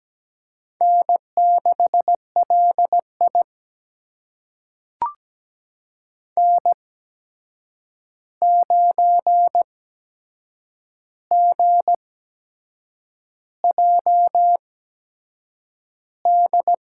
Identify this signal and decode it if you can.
Morse code, heard as a keyed tone, about 17 wpm, spaced as for 7 wpm, 700 Hz. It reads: N6LI N9GJD
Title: USA Call Signs - 8x Character Spacing - 17wpm